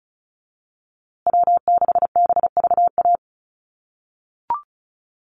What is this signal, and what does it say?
Morse code, heard as a keyed tone, about 34 wpm, 700 Hz. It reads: W6BVA
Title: USA Call Signs Repeated 2x 35wpm